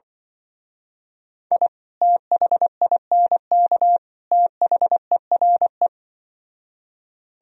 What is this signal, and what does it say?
Morse code, heard as a keyed tone, about 25 wpm, 700 Hz. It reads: I THINK THERE
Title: Speed-Racing - Sets of 3 Words 20wpm